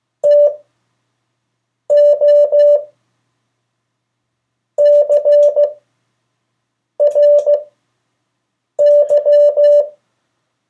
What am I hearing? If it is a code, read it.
Morse code, heard as a keyed tone, about 15 wpm, spaced as for 6 wpm, 580 Hz. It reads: TO CRY
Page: CW Class Review – Delaware Valley Radio Association